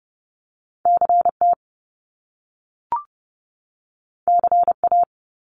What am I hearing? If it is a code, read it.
Morse code, heard as a keyed tone, about 30 wpm, 700 Hz. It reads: CT CA
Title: US States 30wpm